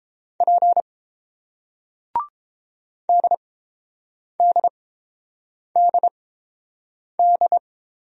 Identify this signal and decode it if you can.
Morse code, heard as a keyed tone, about 30 wpm, spaced as for 7 wpm, 690 Hz. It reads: P DDDD